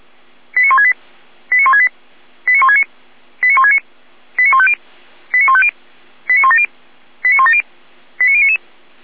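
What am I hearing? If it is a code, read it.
Five-tone selective call (ZVEI), heard as quick runs of five tones, tones of about 70 ms. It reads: 89168 89168 89169 89169 89150 89150 89170 89170 89099
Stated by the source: ZVEI five-tone selective call received on SDR.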